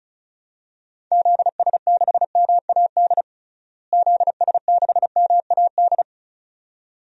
Morse code, heard as a keyed tone, about 34 wpm, 700 Hz. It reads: ZS6MAD ZS6MAD
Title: DXCC-100 Call Signs - 6 Characters - Repeated 2x - 35wpm